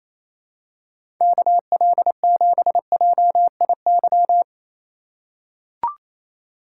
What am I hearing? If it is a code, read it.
Morse code, heard as a keyed tone, about 28 wpm, 700 Hz. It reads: KL7JIY
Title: USA Call Signs - 6 Characters - 4x Character Spacing - 28wpm